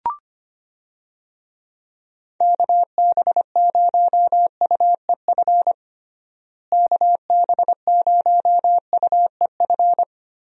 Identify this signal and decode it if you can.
Morse code, heard as a keyed tone, about 25 wpm, 700 Hz. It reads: KB0UEF KB0UEF
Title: USA Call Signs - 6 Characters - Repeated 2x - 25wpm